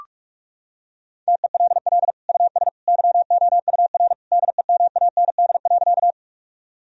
Morse code, heard as a keyed tone, about 46 wpm, 700 Hz. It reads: TELL US YOUR DEMAND.
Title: Sentences from Top 1000 Words - Review Part 4 - 45wpm